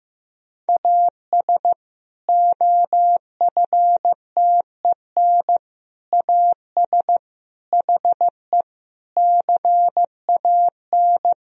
Morse code, heard as a keyed tone, about 15 wpm, 700 Hz. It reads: AS OFTEN AS HE CAN
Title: Sentences from Top 1000 Words - Review Part 1 - 15wpm